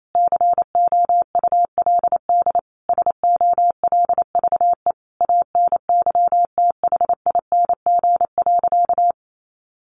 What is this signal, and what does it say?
Morse code, heard as a keyed tone, about 28 wpm, 700 Hz. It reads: COULD SOLVE ANYTHING.
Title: Sentences from Top 1000 Words 28wpm